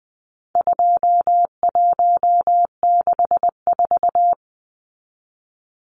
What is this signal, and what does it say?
Morse code, heard as a keyed tone, about 20 wpm, 700 Hz. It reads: 2164
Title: Numbers - 4-Digit 20wpm